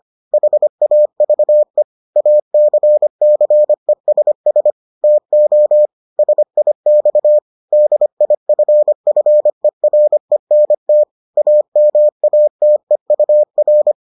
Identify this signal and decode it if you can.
Morse code, heard as a keyed tone, about 25 wpm, 600 Hz. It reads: HAVE ACCESS TO SIX DIFFERENT AMATEUR